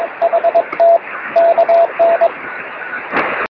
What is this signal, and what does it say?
Morse code, heard as a keyed tone, about 22 wpm, 690 Hz, with noise. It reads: HT KN
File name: sm7lcb / 14-sp4mpb 23cm